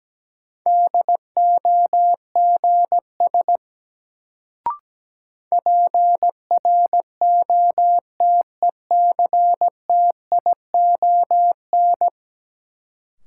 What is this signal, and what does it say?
Morse code, heard as a keyed tone, about 17 wpm, 700 Hz. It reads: DOGS PROTECTION
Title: Sentences from Top 1600 Words - New Word Drill - 17wpm